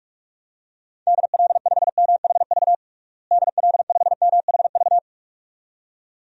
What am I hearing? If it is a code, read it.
Morse code, heard as a keyed tone, about 46 wpm, 700 Hz. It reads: DL5MHV DL5MHV
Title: DXCC-100 Call Signs - 6 Characters - Repeated 2x - 45wpm